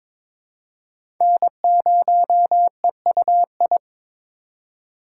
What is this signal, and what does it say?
Morse code, heard as a keyed tone, about 21 wpm, 690 Hz. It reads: N0EUI
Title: USA Call Signs - 5 Characters - 2x Character Spacing - 22wpm